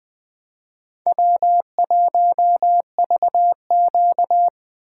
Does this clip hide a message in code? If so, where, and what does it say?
Yes: Morse code from 1.1 to 4.5 s, heard as a keyed tone, about 20 wpm, 700 Hz. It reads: W1VQ